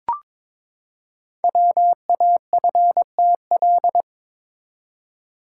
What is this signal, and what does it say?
Morse code, heard as a keyed tone, about 22 wpm, 700 Hz. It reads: WAFTL
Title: Code Groups - 5 Letters - 22wpm